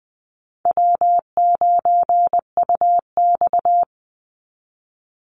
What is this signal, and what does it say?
Morse code, heard as a keyed tone, about 20 wpm, 700 Hz. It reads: W9UX
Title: NA SOTA Chaser Callsigns 20wpm